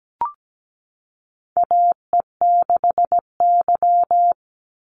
Morse code, heard as a keyed tone, about 17 wpm, 700 Hz. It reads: AE6Y
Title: Call Signs Repeated 3x - 17wpm